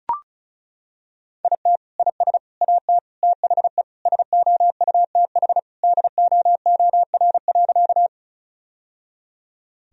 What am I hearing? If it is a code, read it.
Morse code, heard as a keyed tone, about 34 wpm, 700 Hz. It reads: IT IS AT THE SOUTH DOOR.
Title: Sentences from Top 1000 Words - Review Part 2 - 35wpm